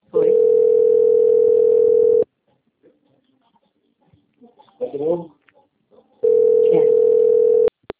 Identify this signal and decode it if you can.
Ringback tone (North America), 440 + 480 Hz, with speech.
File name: out- - - - - .1651